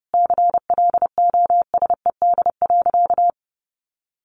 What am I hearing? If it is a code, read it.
Morse code, heard as a keyed tone, about 30 wpm, 700 Hz. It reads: CLOSED.
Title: Sentences from Top 1400 Words 30wpm